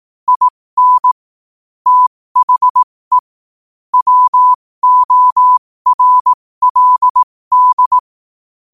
Morse code, heard as a keyed tone, about 18 wpm, spaced as for 15 wpm, 1000 Hz. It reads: IN THE WORLD